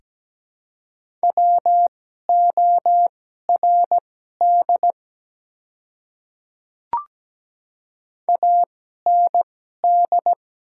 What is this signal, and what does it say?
Morse code, heard as a keyed tone, about 17 wpm, spaced as for 12 wpm, 690 Hz. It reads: WORD AND
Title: 2x Character Spacing - Top 300 Words - 17wpm